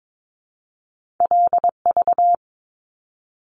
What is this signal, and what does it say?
Morse code, heard as a keyed tone, about 22 wpm, 700 Hz. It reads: LV
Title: Top Two Letter Combinations in Words 22wpm